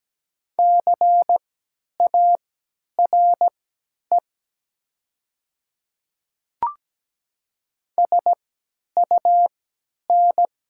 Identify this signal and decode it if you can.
Morse code, heard as a keyed tone, about 17 wpm, spaced as for 10 wpm, 700 Hz. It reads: CARE SUN
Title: 3x Character Spacing - Top 300 Words - 17wpm